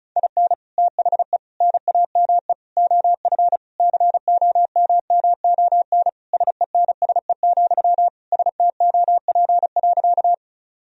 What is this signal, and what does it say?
Morse code, heard as a keyed tone, about 34 wpm, 700 Hz. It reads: IN THE NAME OF COMMON SENSE, STOP.
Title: Sentences from Top 1000 Words - Review Part 4 - 35wpm